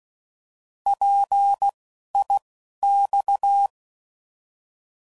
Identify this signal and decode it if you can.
Morse code, heard as a keyed tone, about 16 wpm, spaced as for 12 wpm, 800 Hz. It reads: PIX
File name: Prefixes / By First Letter / Ps